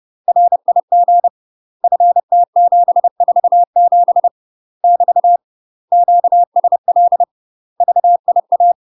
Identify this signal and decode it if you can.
Morse code, heard as a keyed tone, about 30 wpm, 700 Hz. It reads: RIG FT747 <BT> QSL VIA
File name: UK MASTER QSOs TRAINING / 30WPM / UK MASTER QSOs 0027